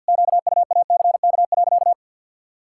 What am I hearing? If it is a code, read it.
Morse code, heard as a keyed tone, about 51 wpm, 700 Hz. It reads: QUACK.
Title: Pangram Challenge v2 - 50wpm